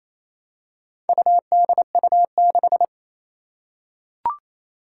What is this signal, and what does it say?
Morse code, heard as a keyed tone, about 27 wpm, 700 Hz. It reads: UDU6
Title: Four Challenging Letter-Number Combinations 28wpm